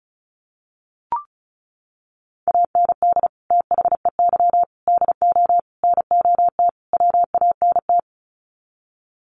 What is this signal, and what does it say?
Morse code, heard as a keyed tone, about 34 wpm, 700 Hz. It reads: AND THEY DO NOT WANT
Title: Sets of 5 Words Encore2 35wpm